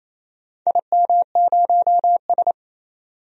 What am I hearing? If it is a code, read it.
Morse code, heard as a keyed tone, about 28 wpm, 700 Hz. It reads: IM0S
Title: DXCC-100 Call Signs - 4 Characters - 28wpm